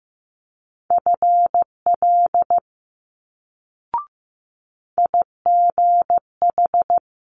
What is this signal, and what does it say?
Morse code, heard as a keyed tone, about 15 wpm, 700 Hz. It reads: FL IGH
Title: Multi-letter Phonemes Quick Fire 15wpm